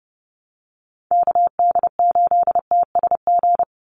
Morse code, heard as a keyed tone, about 28 wpm, 700 Hz. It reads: KD8TSG
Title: USA Call Signs 30wpm